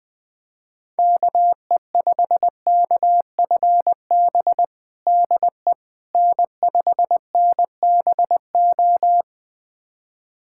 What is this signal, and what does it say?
Morse code, heard as a keyed tone, about 21 wpm, 700 Hz. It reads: KE5KFB DE N5NBO
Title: USA Callsign DE Callsign 20wpm